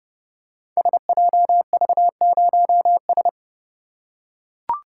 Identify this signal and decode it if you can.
Morse code, heard as a keyed tone, about 30 wpm, 700 Hz. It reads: SJV0S
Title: Code Groups - 5 Letter:Numbers - 30wpm